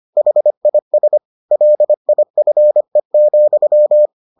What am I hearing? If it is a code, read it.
Morse code, heard as a keyed tone, about 25 wpm, 600 Hz. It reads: HIS LIFE,